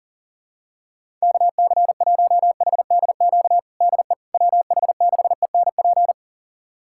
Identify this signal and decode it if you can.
Morse code, heard as a keyed tone, about 40 wpm, 700 Hz. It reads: KC1HDQ DE WH6ENP
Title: USA Callsign DE Callsign - 6 Characters - 40wpm